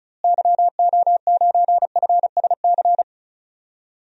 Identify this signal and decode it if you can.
Morse code, heard as a keyed tone, about 36 wpm, 700 Hz. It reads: YO9FSC
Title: DXCC-100 Call Signs - 6 Characters - Repeated 2x - 35wpm